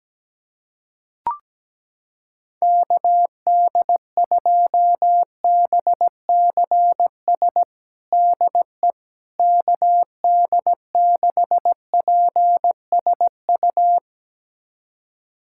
Morse code, heard as a keyed tone, about 17 wpm, 700 Hz. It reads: KD2BCS DE KD6PSU
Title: USA Callsign DE Callsign 17wpm